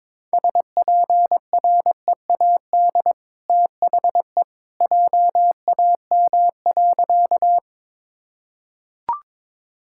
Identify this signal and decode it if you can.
Morse code, heard as a keyed tone, about 22 wpm, 690 Hz. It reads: SPREAD THE JAM.
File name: practice / 22wpm / Easy Sentences with Hints - 22wpm